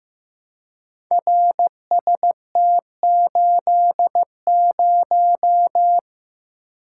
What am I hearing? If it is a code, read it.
Morse code, heard as a keyed tone, about 15 wpm, 700 Hz. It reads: RST80
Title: Code Groups - 5 Letter:Numbers - 15wpm